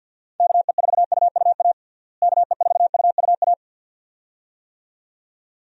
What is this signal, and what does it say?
Morse code, heard as a keyed tone, about 46 wpm, 700 Hz. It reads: KE4UUA KE4UUA
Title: USA Call Signs - 6 Characters - Repeated 2x - 50wpm